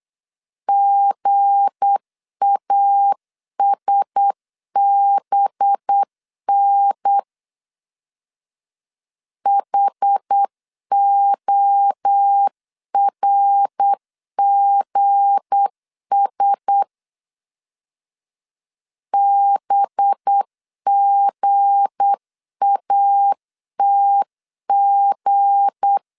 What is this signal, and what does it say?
Morse code, heard as a keyed tone, about 8 wpm, 800 Hz. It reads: GASBN HORGS BGATG
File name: LEZIONI / 7 GF9Z / 7 G 8wpm 3118